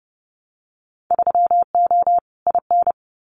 Morse code, heard as a keyed tone, about 30 wpm, 700 Hz. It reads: 3O IN
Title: Winter Field Day Exchanges Encore 30wpm